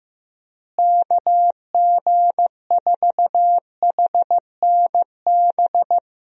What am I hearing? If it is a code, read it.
Morse code, heard as a keyed tone, about 15 wpm, 700 Hz. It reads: KG4HNB
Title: USA Call Signs - 6 Characters - Repeated 3x - 15wpm